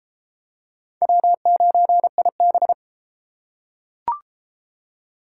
Morse code, heard as a keyed tone, about 30 wpm, 700 Hz. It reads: W9IB